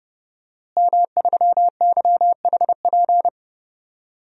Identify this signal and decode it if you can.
Morse code, heard as a keyed tone, about 31 wpm, 700 Hz. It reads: M3YHP
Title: DXCC-100 Call Signs - 5 Characters - Repeated 2x - 30wpm